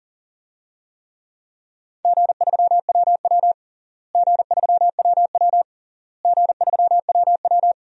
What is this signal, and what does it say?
Morse code, heard as a keyed tone, about 37 wpm, 700 Hz. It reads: G3WW G3WW G3WW
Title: DXCC-100 Call Signs - 4 Characters - Repeated 3x - 40wpm